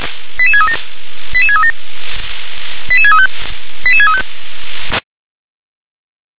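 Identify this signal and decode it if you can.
Five-tone selective call (ZVEI), heard as quick runs of five tones, tones of about 70 ms. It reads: 88637 88637 88635 88635
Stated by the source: ZVEI five-tone selective call 88510, received on an SDR.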